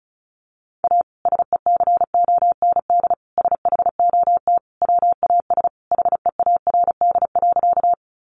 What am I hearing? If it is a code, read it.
Morse code, heard as a keyed tone, about 34 wpm, 700 Hz. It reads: A SECOND SHOT WAS HEARD.
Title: Sentences from Top 1100 Words 35wpm